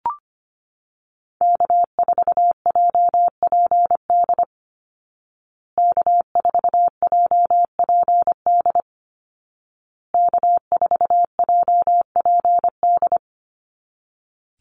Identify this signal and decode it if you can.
Morse code, heard as a keyed tone, about 25 wpm, 700 Hz. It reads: K4JPD K4JPD K4JPD
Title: Call Signs Repeated 2x - 25wpm